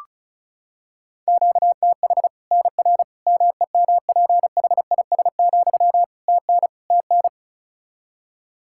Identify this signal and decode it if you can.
Morse code, heard as a keyed tone, about 36 wpm, 700 Hz. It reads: QTH NR MEMPHIS, TN TN
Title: Practice QSO - Exchange 1 - Beginning - 35wpm